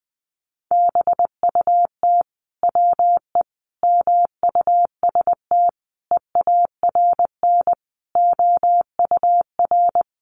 Morse code, heard as a keyed tone, about 19 wpm, 700 Hz. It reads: BUT WE MUST EARN OUR
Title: Sentences from Top 1500 Words 20wpm